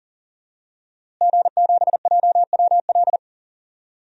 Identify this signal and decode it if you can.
Morse code, heard as a keyed tone, about 40 wpm, 700 Hz. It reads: G7JWL
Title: DXCC-100 Call Signs - 5 Characters - Repeated 3x - 40wpm